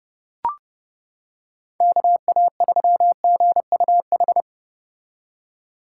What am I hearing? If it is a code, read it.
Morse code, heard as a keyed tone, about 29 wpm, 700 Hz. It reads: KA3GUH